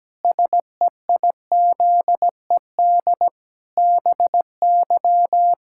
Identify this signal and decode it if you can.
Morse code, heard as a keyed tone, about 17 wpm, 700 Hz. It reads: SEIZED BY
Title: Pangram Challenge v2 - 17wpm